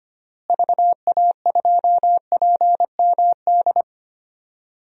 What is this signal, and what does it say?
Morse code, heard as a keyed tone, about 25 wpm, 700 Hz. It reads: VA2PMD
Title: DXCC-100 Call Signs - 6 Characters - Repeated 3x - 25wpm